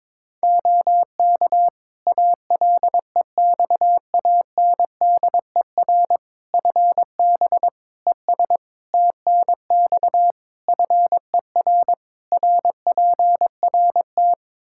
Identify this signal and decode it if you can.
Morse code, heard as a keyed tone, about 22 wpm, 700 Hz. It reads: OK ALEXANDER FB ES TNX FER RPRT
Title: Practice QSO - Exchange 2 - Info - 22wpm